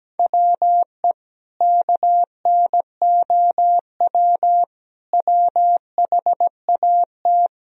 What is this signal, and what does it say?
Morse code, heard as a keyed tone, about 17 wpm, 700 Hz. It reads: WE KNOW WHAT
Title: Sentences from Top 1000 Words - Review Part 4 - 17wpm